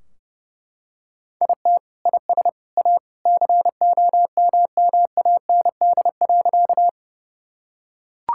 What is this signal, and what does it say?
Morse code, heard as a keyed tone, about 28 wpm, 700 Hz. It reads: IT IS A COMMAND.
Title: Sentences from Top 1600 Words 30wpm